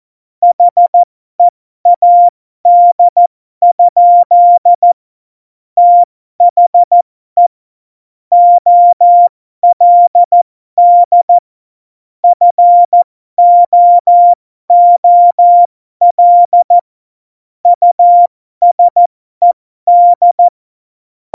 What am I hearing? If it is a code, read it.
Morse code, heard as a keyed tone, about 14 wpm, spaced as for 12 wpm, 700 Hz. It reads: HEAD? THE OLD FOOL USED
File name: Morsekurs / Witze